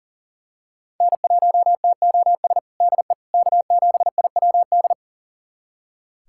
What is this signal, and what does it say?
Morse code, heard as a keyed tone, about 40 wpm, 700 Hz. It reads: N1TOS DE K7IWD
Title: USA Callsign DE Callsign - 5 Characters - 40wpm